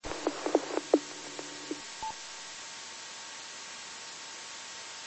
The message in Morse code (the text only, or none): none